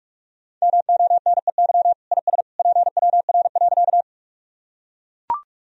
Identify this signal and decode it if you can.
Morse code, heard as a keyed tone, about 41 wpm, 700 Hz. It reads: MONEY IS PWR.
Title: Sentences from Top 500 Words and Abbreviations 45wpm